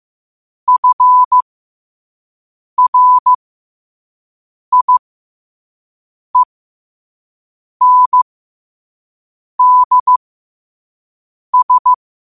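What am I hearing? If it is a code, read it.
Morse code, heard as a keyed tone, about 15 wpm, spaced as for 5 wpm, 1000 Hz. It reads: FRIENDS
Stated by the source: Morse code quote-of-the-day practice at 5 wpm.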